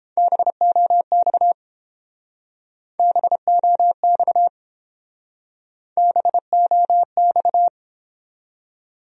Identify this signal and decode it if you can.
Morse code, heard as a keyed tone, about 29 wpm, 700 Hz. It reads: BOX BOX BOX